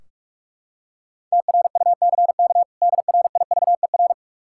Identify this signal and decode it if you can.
Morse code, heard as a keyed tone, about 42 wpm, 700 Hz. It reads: TRUCK DRIVER
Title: Practice QSO - Exchange 3 - More Info - 45wpm